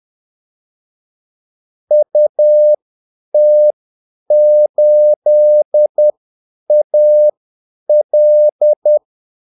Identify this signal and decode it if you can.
Morse code, heard as a keyed tone, about 10 wpm, spaced as for 8 wpm, 600 Hz. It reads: UT8AL
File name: Callsigns-10-8wpm